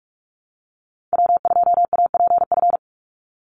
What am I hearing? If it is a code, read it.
Morse code, heard as a keyed tone, about 44 wpm, 700 Hz. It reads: W2APF
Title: Call Signs Repeated 3x - 45wpm